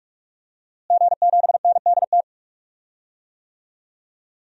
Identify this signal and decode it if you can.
Morse code, heard as a keyed tone, about 45 wpm, 700 Hz. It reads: G7NDT
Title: DXCC-100 Call Signs - 5 Characters - Repeated 2x - 45wpm